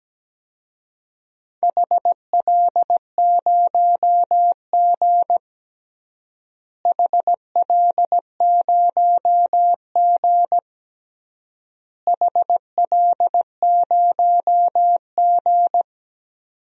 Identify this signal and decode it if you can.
Morse code, heard as a keyed tone, about 17 wpm, 700 Hz. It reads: HL0G HL0G HL0G
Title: DXCC-100 Call Signs - 4 Characters - Repeated 3x - 17wpm